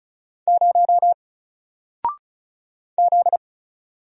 Morse code, heard as a keyed tone, about 36 wpm, 700 Hz. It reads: 0 Z
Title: Single Letter-Number Alt 35wpm